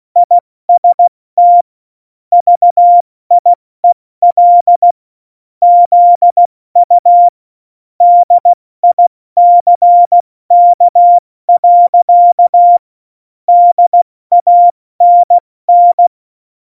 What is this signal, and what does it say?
Morse code, heard as a keyed tone, about 16 wpm, 700 Hz. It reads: IST VIEL ZU DICK. DANN
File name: Morsekurs / Witze / Witz 23. April 2019, 70 F80